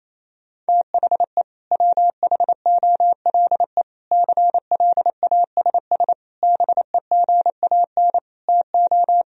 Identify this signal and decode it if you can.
Morse code, heard as a keyed tone, about 28 wpm, 700 Hz. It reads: THE WHOLE CLASS BEGAN TO
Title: Sentences from Top 1000 Words - Review Part 2 - 28wpm